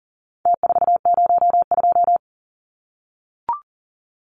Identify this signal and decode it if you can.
Morse code, heard as a keyed tone, about 40 wpm, 700 Hz. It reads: T402
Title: Numbers - Contest Style - 4-Digit 40wpm